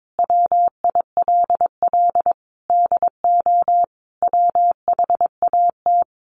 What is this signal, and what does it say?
Morse code, heard as a keyed tone, about 22 wpm, 700 Hz. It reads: WILL DO WHAT
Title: Sentences from Top 900 Words 22wpm